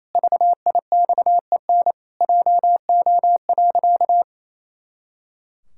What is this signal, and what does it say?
Morse code, heard as a keyed tone, about 28 wpm, 700 Hz. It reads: VIXEN JO.
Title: Pangram Challenge v2 - 28wpm